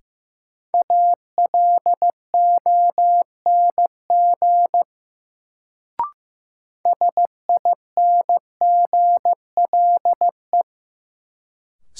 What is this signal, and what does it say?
Morse code, heard as a keyed tone, about 16 wpm, 700 Hz. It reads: ALONG SINGLE
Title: Sentences from Top 700 Words - New Word Drill - 15wpm